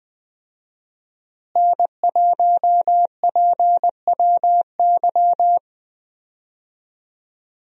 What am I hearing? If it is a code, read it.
Morse code, heard as a keyed tone, about 21 wpm, 700 Hz. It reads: N1PWY
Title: USA Call Signs - 5 Characters - Repeated 2x - 20wpm